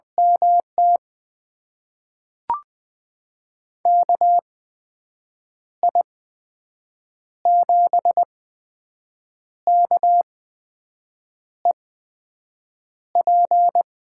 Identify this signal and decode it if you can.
Morse code, heard as a keyed tone, about 20 wpm, spaced as for 6 wpm, 700 Hz. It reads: MT KI7KEP
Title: USA Call Signs - 6 Characters - 8x Character Spacing - 20wpm